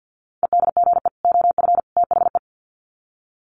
Morse code, heard as a keyed tone, about 50 wpm, 700 Hz. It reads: EDGE OF THE
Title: Sets of 3 Words 50wpm